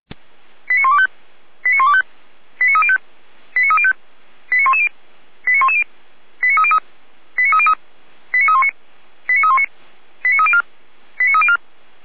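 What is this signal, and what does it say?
Five-tone selective call (ZVEI), heard as quick runs of five tones, tones of about 70 ms. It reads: 89125 89125 89305 89305 89110 89110 89303 89303 89319 89319 89304 89304
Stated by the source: ZVEI five-tone call received by SDR.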